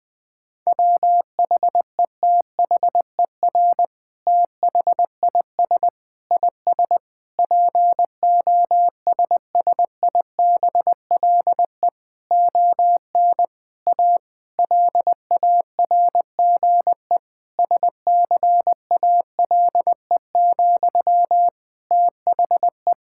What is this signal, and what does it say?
Morse code, heard as a keyed tone, about 20 wpm, 700 Hz. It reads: WHETHER THIS IS POSSIBLE ON A LARGE SCALE, THE
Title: Sentences from Top 1000 Words - Review Part 5 - 20wpm